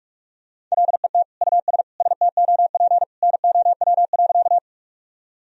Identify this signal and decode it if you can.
Morse code, heard as a keyed tone, about 46 wpm, 700 Hz. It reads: LET US STOP NOW.
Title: Sentences from Top 1000 Words - Review Part 1 - 45wpm